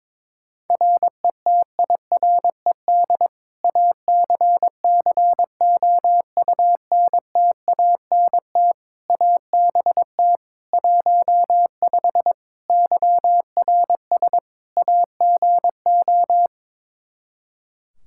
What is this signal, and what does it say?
Morse code, heard as a keyed tone, about 22 wpm, 700 Hz. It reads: RETIRED ACCOUNTANT ABT 15 YRS AGO
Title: Practice QSO - Exchange 3 - More Info - 22wpm